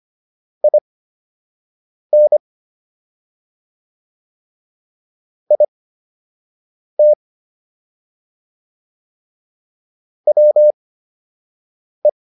Morse code, heard as a keyed tone, about 25 wpm, spaced as for 6 wpm, 600 Hz. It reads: IN IT WE